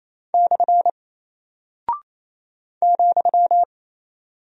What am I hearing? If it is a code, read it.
Morse code, heard as a keyed tone, about 28 wpm, 700 Hz. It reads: / ,
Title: Single Character (Letter:Number:Punctuation:Prosign) Alt 28wpm